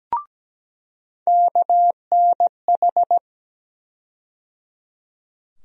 Morse code, heard as a keyed tone, about 17 wpm, 700 Hz. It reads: KNH
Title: USA Call Sign Postfixes - 3 Letters 17wpm